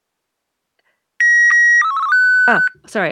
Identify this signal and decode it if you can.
SSTV (slow-scan television), Martin 1, with speech.